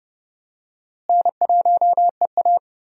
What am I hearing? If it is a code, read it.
Morse code, heard as a keyed tone, about 30 wpm, 700 Hz. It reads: N1EA
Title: USA Call Signs - 4 Characters - Repeated 2x - 30wpm